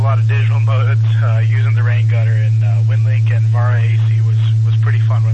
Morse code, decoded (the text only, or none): none